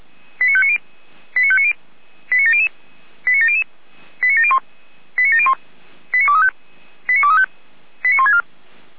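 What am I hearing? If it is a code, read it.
Five-tone selective call (ZVEI), heard as quick runs of five tones, tones of about 70 ms. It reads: 89590 89590 89700 89700 89701 89701 89235 89235 89164
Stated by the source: ZVEI five-tone call received on SDR.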